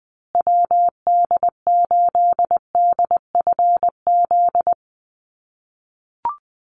Morse code, heard as a keyed tone, about 20 wpm, 700 Hz. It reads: WD8DFZ
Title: USA Call Signs Repeated 2x 20wpm